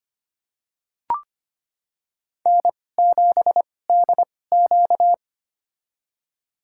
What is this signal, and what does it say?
Morse code, heard as a keyed tone, about 25 wpm, spaced as for 18 wpm, 700 Hz. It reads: N7DQ
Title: USA Call Signs - 4 Characters - 2x Character Spacing - 25wpm